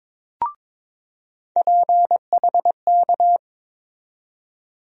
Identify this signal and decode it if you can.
Morse code, heard as a keyed tone, about 21 wpm, 700 Hz. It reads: PHK
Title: USA Call Sign Postfixes - 3 Letters 22wpm